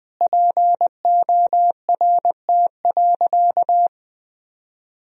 Morse code, heard as a keyed tone, about 20 wpm, 700 Hz. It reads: PORT.
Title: Sentences from Top 1000 Words - Review Part 1 - 20wpm